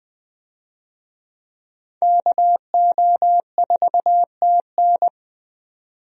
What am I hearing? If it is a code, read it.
Morse code, heard as a keyed tone, about 20 wpm, 700 Hz. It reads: KO4TN
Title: USA Call Signs - 5 Characters - 20wpm